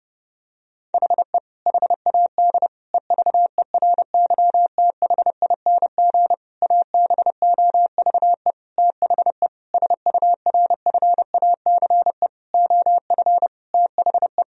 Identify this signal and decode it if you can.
Morse code, heard as a keyed tone, about 30 wpm, 700 Hz. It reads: HE HAD EVERYTHING ABOVE THE SURFACE OF THE
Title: Sentences from Top 1000 Words - Review Part 3 - 30wpm